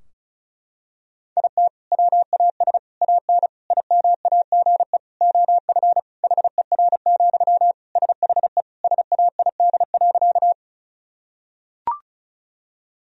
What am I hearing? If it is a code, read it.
Morse code, heard as a keyed tone, about 36 wpm, 700 Hz. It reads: IT WAS AN IMAGE OF HER, SHE SAID.
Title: Sentences from Top 1000 Words - Review Part 4 - 35wpm